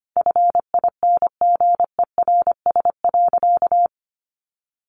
Morse code, heard as a keyed tone, about 25 wpm, 700 Hz. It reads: FINGERS.
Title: A Sound Of Thunder 25wpm